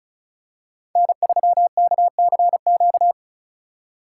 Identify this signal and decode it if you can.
Morse code, heard as a keyed tone, about 36 wpm, 700 Hz. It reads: N3KCQ
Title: USA Call Signs - 5 Characters - 3x Character Spacing - 35wpm